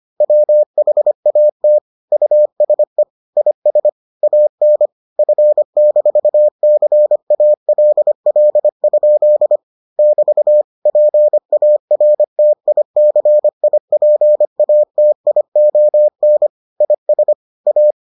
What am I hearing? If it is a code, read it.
Morse code, heard as a keyed tone, about 25 wpm, 600 Hz. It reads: WHAT USE IS AN F-CALL? <BT> PARTICIPATION IS A